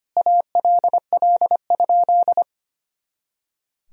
Morse code, heard as a keyed tone, about 25 wpm, 700 Hz. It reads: ALL?
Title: Sentences from Top 200 Words and Abbreviations 25wpm